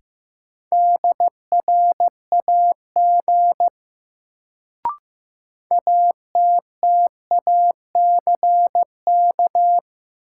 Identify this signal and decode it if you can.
Morse code, heard as a keyed tone, about 15 wpm, 700 Hz. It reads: DRAG ATTACK
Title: Sentences from Top 1500 Words - New Word Drill Alt - 15wpm